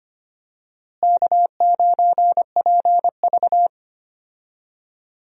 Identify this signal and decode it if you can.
Morse code, heard as a keyed tone, about 25 wpm, 700 Hz. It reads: K9PV
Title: USA Call Signs - 4 Characters - Repeated 2x - 25wpm